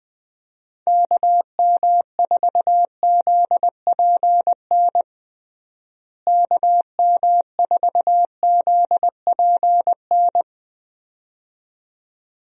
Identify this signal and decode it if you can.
Morse code, heard as a keyed tone, about 20 wpm, 700 Hz. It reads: KM4ZPN KM4ZPN
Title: USA Call Signs - 6 Characters - Repeated 2x - 20wpm